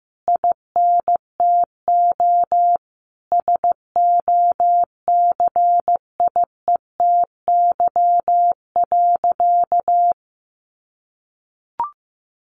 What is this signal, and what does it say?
Morse code, heard as a keyed tone, about 15 wpm, 700 Hz. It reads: INTO SOCIETY.
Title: Sentences from Top 700 Words 15wpm